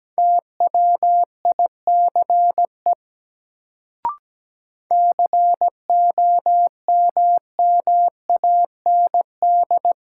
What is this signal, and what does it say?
Morse code, heard as a keyed tone, about 17 wpm, 700 Hz. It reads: TWICE COMMAND
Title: Sentences from Top 1600 Words - New Word Drill Alt - 17wpm